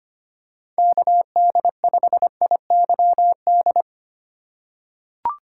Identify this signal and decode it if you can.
Morse code, heard as a keyed tone, about 25 wpm, 700 Hz. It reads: KD5IYD
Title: USA Call Signs - 6 Characters - 4x Character Spacing - 25wpm